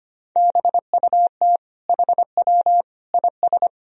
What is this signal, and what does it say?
Morse code, heard as a keyed tone, about 25 wpm, 700 Hz. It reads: BUT HW IS
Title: Sentences from Top 100 Words and Abbreviations 25wpm